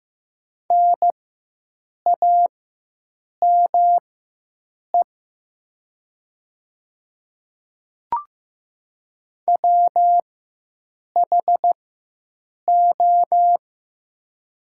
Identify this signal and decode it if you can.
Morse code, heard as a keyed tone, about 15 wpm, spaced as for 7 wpm, 700 Hz. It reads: NAME WHO
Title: 4x Character Spacing - Top 200 Words - 15wpm